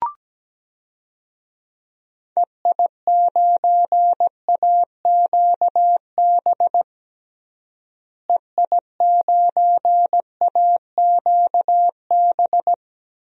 Morse code, heard as a keyed tone, about 17 wpm, 700 Hz. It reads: EI9AQB EI9AQB
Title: DXCC-100 Call Signs - 6 Characters - Repeated 2x - 17wpm